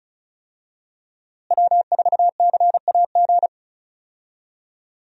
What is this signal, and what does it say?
Morse code, heard as a keyed tone, about 36 wpm, 700 Hz. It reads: W4CAG
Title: USA Call Signs - 5 Characters - Repeated 2x - 35wpm